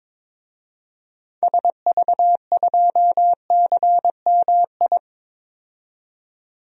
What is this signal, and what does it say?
Morse code, heard as a keyed tone, about 22 wpm, 700 Hz. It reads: SV2CMI
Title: DXCC-100 Call Signs - 6 Characters - 22wpm